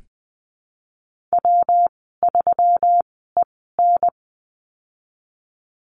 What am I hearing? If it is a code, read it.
Morse code, heard as a keyed tone, about 19 wpm, spaced as for 14 wpm, 700 Hz. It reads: W3EN
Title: 154 Lesson 25g - Letter p - Call Signs 20wpm